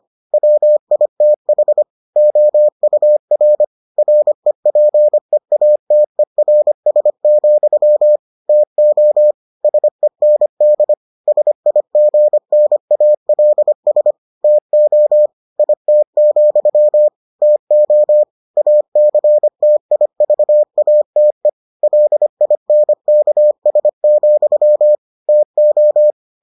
Morse code, heard as a keyed tone, about 25 wpm, 600 Hz. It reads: WITH OUR REPEATERS, TO SEND SIGNALS TO IT, TO ACTIVATE LINKS, TO